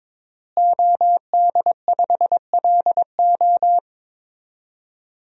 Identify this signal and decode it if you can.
Morse code, heard as a keyed tone, about 22 wpm, 700 Hz. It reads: OD5LO
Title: DXCC-100 Call Signs - 5 Characters - Repeated 3x - 22wpm